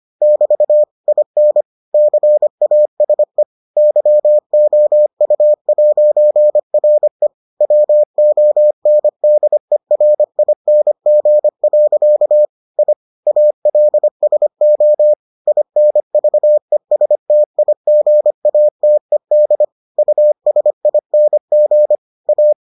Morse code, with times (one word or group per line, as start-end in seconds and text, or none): 0.2-0.8 <BT>
1.1-1.6 IN
1.9-3.4 CASE
3.8-7.3 YOU'RE
7.6-12.4 WONDERING.
12.8-12.9 I
13.3-15.1 ALSO
15.5-19.6 INVESTIGATED
20.0-21.9 USING
22.3-22.5 A